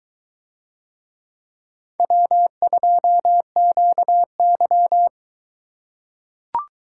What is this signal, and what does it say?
Morse code, heard as a keyed tone, about 22 wpm, 700 Hz. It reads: W2QY